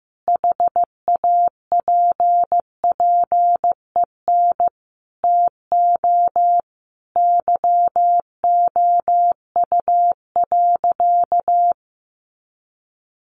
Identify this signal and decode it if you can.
Morse code, heard as a keyed tone, about 15 wpm, 700 Hz. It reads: HAPPEN TO YOU.
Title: Sentences from Top 500 Words 15wpm